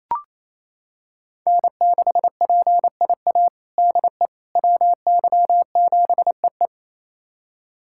Morse code, heard as a keyed tone, about 28 wpm, 700 Hz. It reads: N6PIA DE WY7EE
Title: USA Callsign DE Callsign - 5 Characters - 28wpm